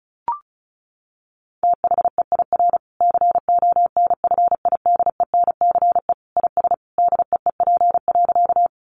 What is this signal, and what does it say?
Morse code, heard as a keyed tone, about 34 wpm, 700 Hz. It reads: THEIR CONFIDENCE IS DEEP.
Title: Sentences from Top 1500 Words 35wpm